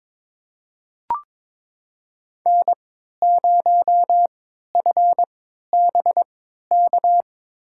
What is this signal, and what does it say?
Morse code, heard as a keyed tone, about 21 wpm, spaced as for 12 wpm, 700 Hz. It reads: N0FBK
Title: USA Call Signs - 3x Character Spacing - 22wpm